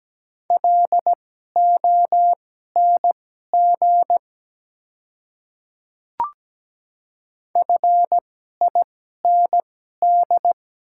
Morse code, heard as a keyed tone, about 17 wpm, spaced as for 12 wpm, 700 Hz. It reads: LONG FIND
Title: 2x Character Spacing - Top 300 Words - 17wpm